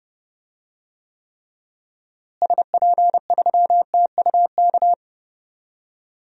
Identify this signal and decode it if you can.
Morse code, heard as a keyed tone, about 29 wpm, 700 Hz. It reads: SP3TUK
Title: DXCC-100 Call Signs - 6 Characters - Repeated 3x - 30wpm